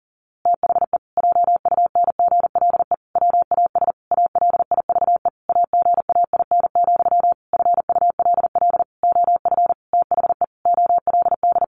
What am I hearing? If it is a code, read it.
Morse code, heard as a keyed tone, about 40 wpm, 700 Hz. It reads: THE JUNGLE WAS ALIVE AGAIN, FULL OF THE OLD
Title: A Sound Of Thunder 40wpm